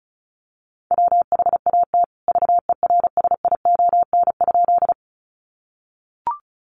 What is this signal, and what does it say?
Morse code, heard as a keyed tone, about 34 wpm, 700 Hz. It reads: WHAT VERSION?
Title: Sentences from Top 1400 Words 35wpm